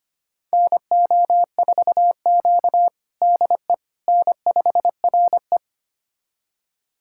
Morse code, heard as a keyed tone, about 25 wpm, 700 Hz. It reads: NO4Q DE N5RE
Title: USA Callsign DE Callsign - 4 Characters - 25wpm